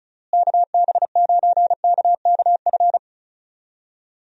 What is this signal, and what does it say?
Morse code, heard as a keyed tone, about 36 wpm, 700 Hz. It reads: KB9KKF
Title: USA Call Signs - 6 Characters - Repeated 2x - 35wpm